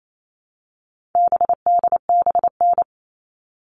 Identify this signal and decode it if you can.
Morse code, heard as a keyed tone, about 28 wpm, 700 Hz. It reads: BDBN
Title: All Cut Numbers - 4-Digit 28wpm